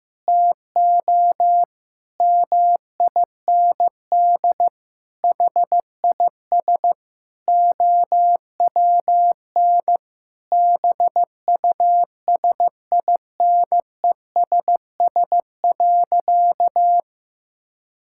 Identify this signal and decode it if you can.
Morse code, heard as a keyed tone, about 16 wpm, 700 Hz. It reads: TO MIND HIS OWN BUSINESS.
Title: Sentences from Top 1000 Words - Review Part 6 - 15wpm